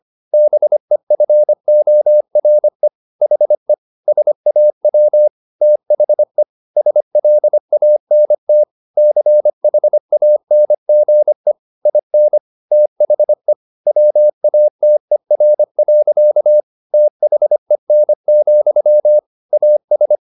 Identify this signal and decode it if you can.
Morse code, heard as a keyed tone, about 25 wpm, 600 Hz. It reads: BEFORE HE SAW THE SLANT CHANGE IN THE WATER. THEN, AS
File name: the old man and the sea part 1 25wpm